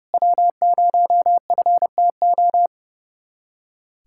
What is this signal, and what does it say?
Morse code, heard as a keyed tone, about 32 wpm, 700 Hz. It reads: W0FTO
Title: USA Callsign DE Callsign - 5 Characters - 30wpm